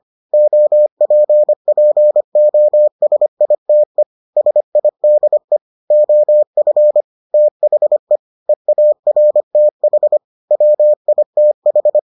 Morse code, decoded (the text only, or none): OPPOSITE SIDE OF THE EARTH WITH